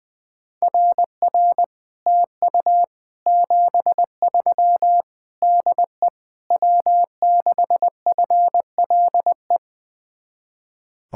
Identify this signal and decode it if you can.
Morse code, heard as a keyed tone, about 21 wpm, 700 Hz. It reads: RR TU 73 DE W6FLE
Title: Speed-Racing Common QSO Phrases - 20wpm